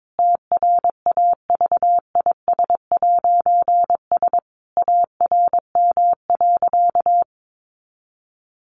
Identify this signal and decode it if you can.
Morse code, heard as a keyed tone, about 22 wpm, 700 Hz. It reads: TRAVIS'S ARM.
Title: A Sound Of Thunder 22wpm